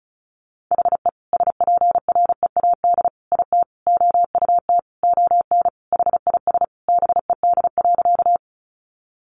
Morse code, heard as a keyed tone, about 34 wpm, 700 Hz. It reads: HE SPREAD IT OUT ON HIS BED.
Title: Sentences from Top 1200 Words 35wpm